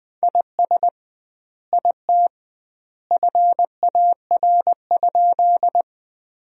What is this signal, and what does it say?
Morse code, heard as a keyed tone, about 20 wpm, 700 Hz. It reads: IS IT FAR?
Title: 2x Word-Spacing - Sentences from Top 300 Words - 20wpm